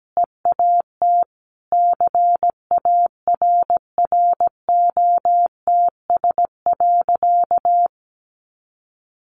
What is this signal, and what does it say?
Morse code, heard as a keyed tone, about 17 wpm, 700 Hz. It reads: EAT CARROTS.
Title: Easy Phrases 17wpm